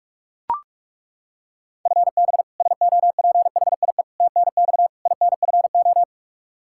Morse code, heard as a keyed tone, about 41 wpm, 700 Hz. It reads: FB SOPHIE TNX INFO
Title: Practice QSO - Exchange 3 - More Info - 45wpm